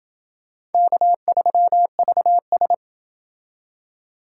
Morse code, heard as a keyed tone, about 28 wpm, 700 Hz. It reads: K3VS